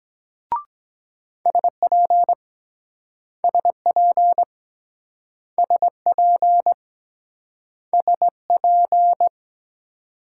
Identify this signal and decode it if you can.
Morse code, heard as a keyed tone, about 24 wpm, 700 Hz. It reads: SP SP SP SP